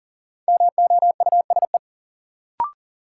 Morse code, heard as a keyed tone, about 40 wpm, 700 Hz. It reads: MOUSE
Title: Sentences from Top 700 Words - New Word Drill - 40wpm